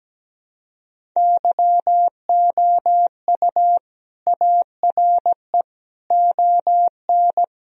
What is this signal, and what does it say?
Morse code, heard as a keyed tone, about 17 wpm, 700 Hz. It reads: YOU ARE ON
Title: Sentences from Top 1000 Words - Review Part 1 - 17wpm